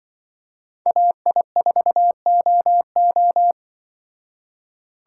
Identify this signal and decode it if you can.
Morse code, heard as a keyed tone, about 24 wpm, 700 Hz. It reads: AI4OO
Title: Speed Racing - USA Call Signs 20wpm